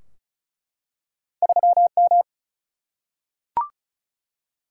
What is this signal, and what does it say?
Morse code, heard as a keyed tone, about 36 wpm, 700 Hz. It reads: 3M
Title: Single-Letter Single-Number 35wpm